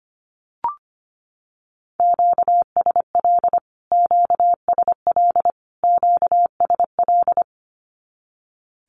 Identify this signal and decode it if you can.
Morse code, heard as a keyed tone, about 25 wpm, 700 Hz. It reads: QSL QSL QSL
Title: Q-Codes Repeated 3x 25wpm